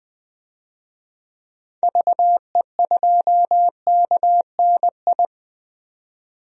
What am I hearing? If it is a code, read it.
Morse code, heard as a keyed tone, about 20 wpm, 700 Hz. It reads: VE2KNI
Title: DXCC-100 Call Signs - 6 Characters - 20wpm